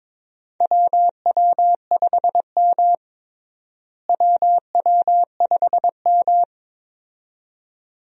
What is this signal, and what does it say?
Morse code, heard as a keyed tone, about 22 wpm, 700 Hz. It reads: WW5M WW5M
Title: USA Call Signs - 4 Characters - Repeated 2x - 22wpm